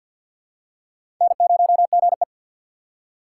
Morse code, heard as a keyed tone, about 47 wpm, 700 Hz. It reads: N0GE
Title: USA Call Signs - 3x Character Spacing - 50wpm